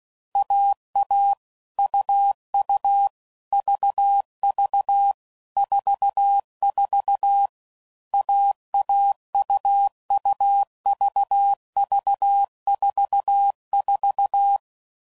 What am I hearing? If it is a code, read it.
Morse code, heard as a keyed tone, about 16 wpm, 800 Hz. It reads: AA UU VV 44 AAUUVV44
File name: Combinations / Difficult Combos / AUV AUV4